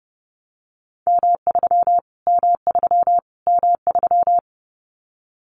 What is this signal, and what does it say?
Morse code, heard as a keyed tone, about 30 wpm, 700 Hz. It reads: M3 M3 M3
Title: Callsign Prefixes for Top 25 DXCC - Repeated 3x - 30wpm